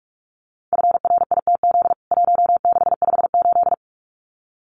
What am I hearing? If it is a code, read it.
Morse code, heard as a keyed tone, about 44 wpm, 700 Hz. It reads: FRITZ 1658
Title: CWT Exchange 45wpm